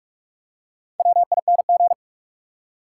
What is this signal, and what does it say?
Morse code, heard as a keyed tone, about 48 wpm, 700 Hz. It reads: WING
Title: Four Letter Words - 45wpm